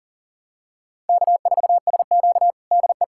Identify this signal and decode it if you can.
Morse code, heard as a keyed tone, about 37 wpm, 700 Hz. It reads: K4SQ DE
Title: USA Callsign DE Callsign - 4 Characters - 40wpm